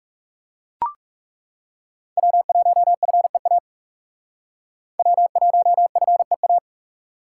Morse code, heard as a keyed tone, about 36 wpm, 700 Hz. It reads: W1FEA W1FEA
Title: Speed Racing - USA 1x3 Call Signs 30wpm